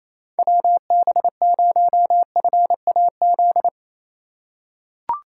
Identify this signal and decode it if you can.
Morse code, heard as a keyed tone, about 28 wpm, 700 Hz. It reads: WB0FAZ
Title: USA Call Signs - 6 Characters - 4x Character Spacing - 28wpm